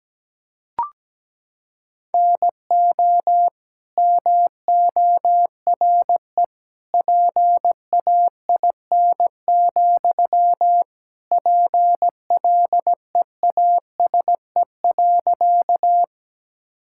Morse code, heard as a keyed tone, about 17 wpm, 700 Hz. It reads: NO MORE PAIN, PLEASE.
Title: Sentences from Top 1600 Words 17wpm